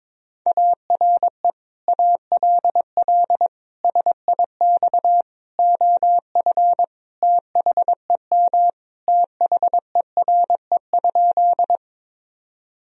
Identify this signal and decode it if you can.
Morse code, heard as a keyed tone, about 21 wpm, 700 Hz. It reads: ARE ALL SIX OF THEM THERE?
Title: Sentences from Top 1000 Words - Review Part 2 - 22wpm